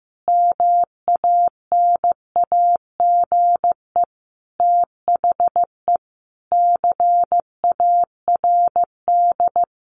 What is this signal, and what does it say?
Morse code, heard as a keyed tone, about 15 wpm, 700 Hz. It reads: MANAGE THE CARD
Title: Sentences from Top 1500 Words 15wpm